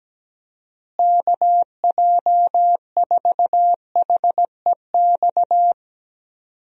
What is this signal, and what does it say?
Morse code, heard as a keyed tone, about 17 wpm, 700 Hz. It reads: KJ4HEX
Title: USA Call Signs - 6 Characters - Repeated 2x - 17wpm